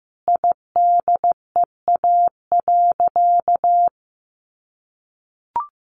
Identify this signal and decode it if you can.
Morse code, heard as a keyed tone, about 15 wpm, 700 Hz. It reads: IDEA.
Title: Sentences from Top 1000 Words 15wpm